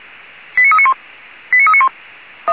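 Five-tone selective call (ZVEI), heard as quick runs of five tones, tones of about 70 ms. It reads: 89391 89391
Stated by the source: ZVEI five-tone selective call received on an SDR.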